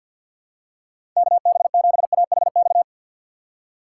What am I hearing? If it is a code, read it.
Morse code, heard as a keyed tone, about 47 wpm, 700 Hz. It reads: KB7AHX
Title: USA Call Signs - 6x Character Spacing - 50wpm